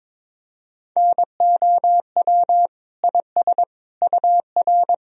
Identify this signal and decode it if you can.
Morse code, heard as a keyed tone, about 22 wpm, 700 Hz. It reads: NOW IS UR
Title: Sentences from Top 100 Words and Abbreviations 22wpm